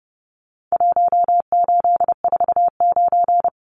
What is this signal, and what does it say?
Morse code, heard as a keyed tone, about 30 wpm, 700 Hz. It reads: 1849
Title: Numbers - Contest Style - 4-Digit 30wpm